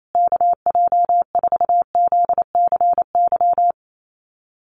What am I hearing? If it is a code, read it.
Morse code, heard as a keyed tone, about 28 wpm, 700 Hz. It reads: KJ4ZCY
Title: USA Call Signs Repeated 2x 28wpm